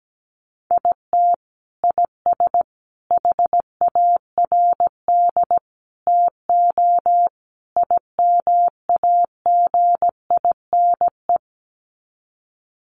Morse code, heard as a keyed tone, about 17 wpm, 700 Hz. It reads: IT IS HARD TO IMAGINE
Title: Sets of 5 Words Encore2 17wpm